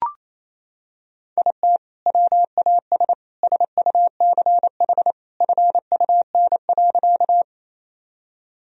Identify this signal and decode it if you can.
Morse code, heard as a keyed tone, about 28 wpm, 700 Hz. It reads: IT WAS SUCH FUN.
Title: Sentences from Top 1000 Words - Review Part 4 - 28wpm